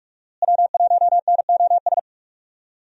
Morse code, heard as a keyed tone, about 42 wpm, 700 Hz. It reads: W1NOS
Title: USA Call Signs - 5 Characters - Repeated 2x - 45wpm